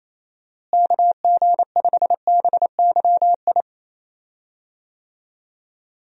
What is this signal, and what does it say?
Morse code, heard as a keyed tone, about 28 wpm, 700 Hz. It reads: KG5BYI
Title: USA Call Signs - 6 Characters - Repeated 3x - 28wpm